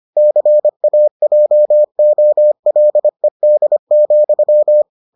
Morse code, heard as a keyed tone, about 25 wpm, 600 Hz. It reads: CAJOLED,